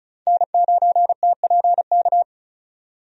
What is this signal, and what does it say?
Morse code, heard as a keyed tone, about 36 wpm, 700 Hz. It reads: N9TPK
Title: USA Call Signs - 5 Characters - 35wpm